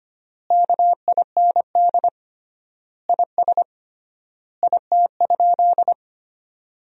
Morse code, heard as a keyed tone, about 25 wpm, 700 Hz. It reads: KIND IS IT?
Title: 3x Word-Spacing - Sentences from Top 200 Words - 25wpm